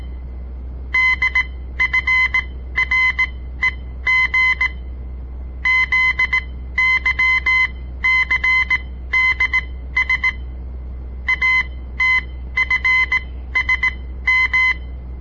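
Morse code, heard as a keyed tone, about 18 wpm, spaced as for 13 wpm, 2050 Hz. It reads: DFREG ZYCDS ATFSM
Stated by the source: On-air Morse code practice broadcast by a southern Montana amateur radio association; all code, no read back.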